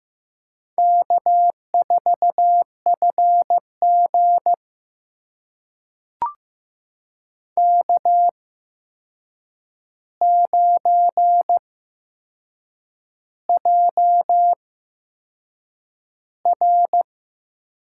Morse code, heard as a keyed tone, about 15 wpm, spaced as for 6 wpm, 700 Hz. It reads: K4FG K9JR
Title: USA Call Signs - 4 Characters - 8x Character Spacing - 15wpm